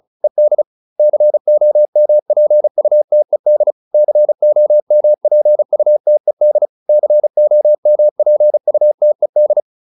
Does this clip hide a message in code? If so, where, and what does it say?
Yes: Morse code from 0.2 to 9.6 s, heard as a keyed tone, about 34 wpm, 600 Hz. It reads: ED COMPUTED COMPUTED COMPUTED